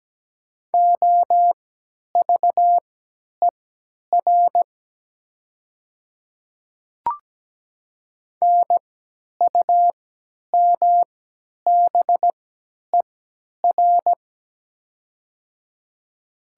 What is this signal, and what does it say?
Morse code, heard as a keyed tone, about 17 wpm, spaced as for 10 wpm, 700 Hz. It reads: OVER NUMBER
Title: 3x Character Spacing - Top 200 Words - 17wpm